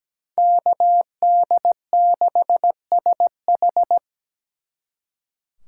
Morse code, heard as a keyed tone, about 20 wpm, 700 Hz. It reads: KD6SH
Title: Speed Racing - USA 2x2 Call Signs 17wpm